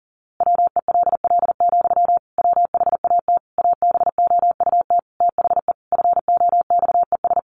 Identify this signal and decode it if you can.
Morse code, heard as a keyed tone, about 40 wpm, 700 Hz. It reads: WELL, WHAT ABOUT THE FOXES
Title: A Sound Of Thunder 40wpm